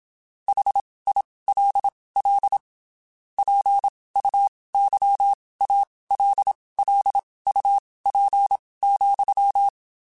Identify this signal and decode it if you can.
Morse code, heard as a keyed tone, about 27 wpm, spaced as for 19 wpm, 800 Hz. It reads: HILL PUYALLUP,
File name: QSOs Spaced